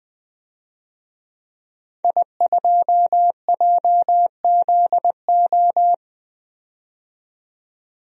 Morse code, heard as a keyed tone, about 20 wpm, 700 Hz. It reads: I2JZO
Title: DXCC-100 Call Signs - 5 Characters - 20wpm